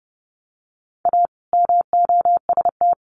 Morse code, heard as a keyed tone, about 28 wpm, 700 Hz. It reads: A MOST
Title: Sentences from Top 1100 Words 30wpm